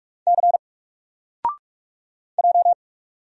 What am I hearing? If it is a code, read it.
Morse code, heard as a keyed tone, about 44 wpm, 700 Hz. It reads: C J
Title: Single Character (Letter:Number:Punctuation:Prosign) 45wpm